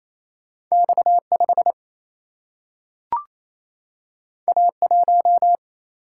Morse code, heard as a keyed tone, about 28 wpm, 700 Hz. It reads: X5 A1
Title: Single-Letter Single-Number 28wpm